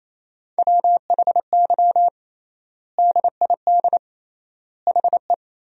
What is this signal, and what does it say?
Morse code, heard as a keyed tone, about 28 wpm, 700 Hz. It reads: WHY DID HE
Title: 3x Word-Spacing - Sentences from Top 200 Words - 28wpm